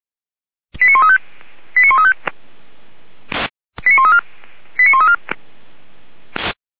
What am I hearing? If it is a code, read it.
Five-tone selective call (ZVEI), heard as quick runs of five tones, tones of about 70 ms. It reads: 89135 89135 89134 89134